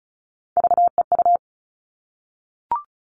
Morse code, heard as a keyed tone, about 34 wpm, 700 Hz. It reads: VEU
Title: All Cut Numbers - 3-Digit 35wpm